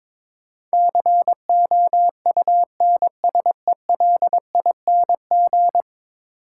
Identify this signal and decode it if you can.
Morse code, heard as a keyed tone, about 22 wpm, 700 Hz. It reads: COUNSELING
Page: morse-code-ninja-practice-22wpm